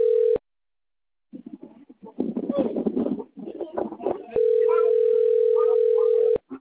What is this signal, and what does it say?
Ringback tone (North America), 440 + 480 Hz.